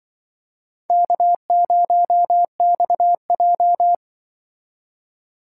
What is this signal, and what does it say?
Morse code, heard as a keyed tone, about 25 wpm, 700 Hz. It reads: K0XJ
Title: Speed Racing - USA 1x2 Call Signs 20wpm